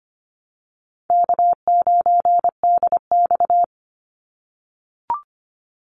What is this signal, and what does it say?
Morse code, heard as a keyed tone, about 25 wpm, 700 Hz. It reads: K9DX
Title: Easy Call Signs - 25wpm